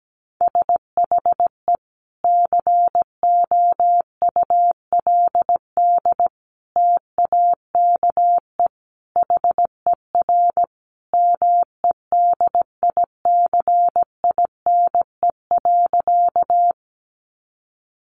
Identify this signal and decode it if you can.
Morse code, heard as a keyed tone, about 17 wpm, 700 Hz. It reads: SHE COULD TAKE HER MEDICINE.
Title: Sentences from Top 900 Words 17wpm